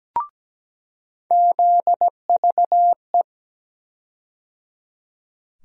Morse code, heard as a keyed tone, about 17 wpm, 700 Hz. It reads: ZVE
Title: USA Call Sign Postfixes - 3 Letters 17wpm